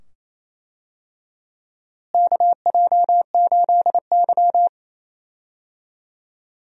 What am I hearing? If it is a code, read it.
Morse code, heard as a keyed tone, about 28 wpm, 700 Hz. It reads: KJ8Y
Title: USA Call Signs - 4 Characters - Repeated 2x - 28wpm